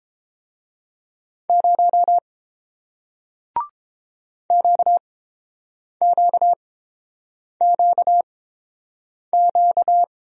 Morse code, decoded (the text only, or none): 0 QQQQ